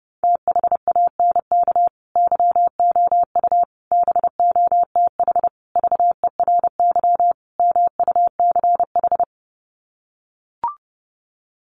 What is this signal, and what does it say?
Morse code, heard as a keyed tone, about 30 wpm, 700 Hz. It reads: THANK YOU BOTH VERY MUCH
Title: Sets of 5 Words 30wpm